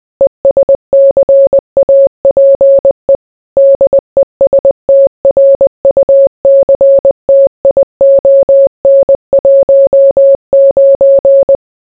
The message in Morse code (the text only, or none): ESCAPE DESTRUCTION 19